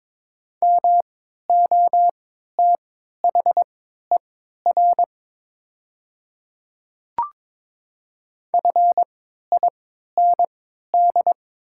Morse code, heard as a keyed tone, about 22 wpm, spaced as for 13 wpm, 700 Hz. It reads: MOTHER FIND
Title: 3x Character Spacing - Top 200 Words - 22wpm